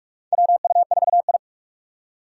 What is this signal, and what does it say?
Morse code, heard as a keyed tone, about 46 wpm, 700 Hz. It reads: WU4I
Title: USA Call Signs - 4 Characters - 6x Character Spacing - 45wpm